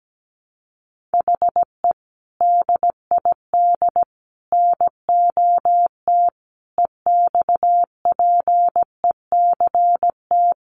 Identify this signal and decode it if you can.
Morse code, heard as a keyed tone, about 17 wpm, 700 Hz. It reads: HE DID NOT EXPECT
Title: Sentences from Top 1100 Words 17wpm